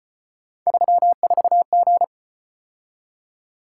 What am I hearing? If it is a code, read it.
Morse code, heard as a keyed tone, about 35 wpm, 700 Hz. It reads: 34G